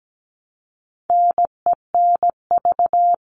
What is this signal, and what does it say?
Morse code, heard as a keyed tone, about 17 wpm, 700 Hz. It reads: NENV
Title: All Cut Numbers - 4-Digit 17wpm